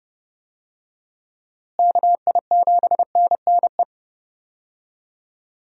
Morse code, heard as a keyed tone, about 29 wpm, 700 Hz. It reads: KI7NNE
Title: USA Call Signs - 6 Characters - Repeated 2x - 30wpm